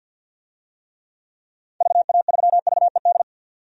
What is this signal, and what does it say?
Morse code, heard as a keyed tone, about 48 wpm, 700 Hz. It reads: VA3VED
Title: DXCC-100 Call Signs - 6 Characters - 50wpm